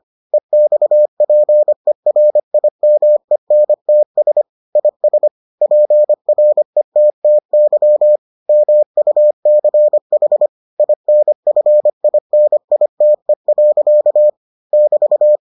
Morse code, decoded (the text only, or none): EXPERIMENTS IS PRETTY MUCH INFINITE. <BT>